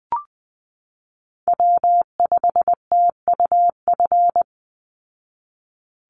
Morse code, heard as a keyed tone, about 20 wpm, 700 Hz. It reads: W5TUF
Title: Call Signs Repeated 3x - 20wpm